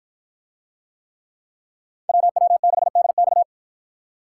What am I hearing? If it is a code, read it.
Morse code, heard as a keyed tone, about 53 wpm, 700 Hz. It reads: WW6DX